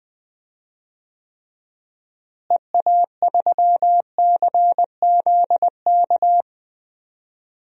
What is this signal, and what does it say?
Morse code, heard as a keyed tone, about 20 wpm, 700 Hz. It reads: EA3CZK
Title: DXCC-100 Call Signs - 6 Characters - Repeated 2x - 20wpm